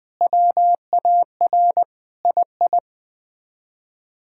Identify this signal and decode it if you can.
Morse code, heard as a keyed tone, about 21 wpm, 700 Hz. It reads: WAR II
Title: Sets of 6 Words - 20wpm